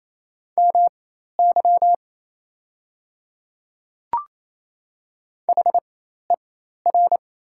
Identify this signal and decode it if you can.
Morse code, heard as a keyed tone, about 28 wpm, spaced as for 13 wpm, 700 Hz. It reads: MY HER
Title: 4x Character Spacing - Top 300 Words - 28wpm